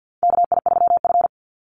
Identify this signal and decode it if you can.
Morse code, heard as a keyed tone, about 50 wpm, 700 Hz. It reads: KI3F
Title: Call Signs - 50wpm